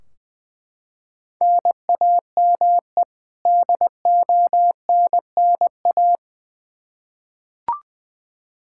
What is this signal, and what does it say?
Morse code, heard as a keyed tone, about 20 wpm, 700 Hz. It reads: NAME DONNA
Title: Practice QSO - Exchange 1 - Beginning - 20wpm